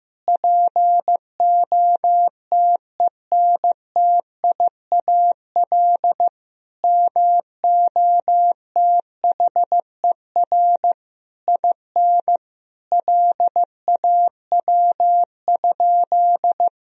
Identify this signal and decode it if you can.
Morse code, heard as a keyed tone, about 15 wpm, 700 Hz. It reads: POTENTIAL MOTHER IN LAW?
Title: Sentences from Top 1000 Words - Review Part 4 - 15wpm